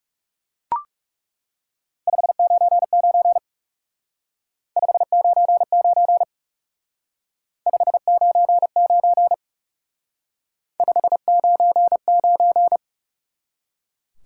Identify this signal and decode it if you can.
Morse code, heard as a keyed tone, about 38 wpm, 700 Hz. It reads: 599 599 599 599